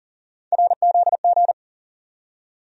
Morse code, heard as a keyed tone, about 40 wpm, 700 Hz. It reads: RZG
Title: USA Call Sign Postfixes - 3 Letters 40wpm